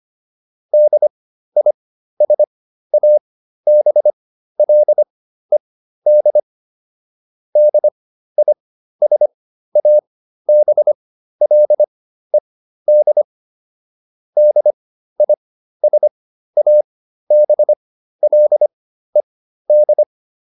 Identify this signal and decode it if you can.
Morse code, heard as a keyed tone, about 25 wpm, spaced as for 13 wpm, 600 Hz. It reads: DISABLED DISABLED DISABLED